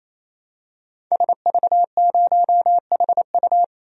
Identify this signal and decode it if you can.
Morse code, heard as a keyed tone, about 28 wpm, 700 Hz. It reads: SV0HU
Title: DXCC-100 Call Signs - 5 Characters - 28wpm